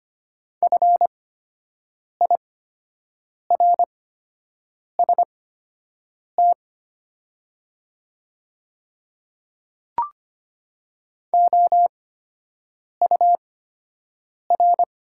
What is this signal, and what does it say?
Morse code, heard as a keyed tone, about 25 wpm, spaced as for 7 wpm, 700 Hz. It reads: FIRST OUR
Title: 8x Character Spacing - Top 200 Words - 25wpm